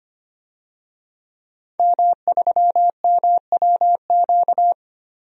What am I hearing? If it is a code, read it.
Morse code, heard as a keyed tone, about 25 wpm, 700 Hz. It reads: M3MWQ
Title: DXCC-100 Call Signs - 5 Characters - 25wpm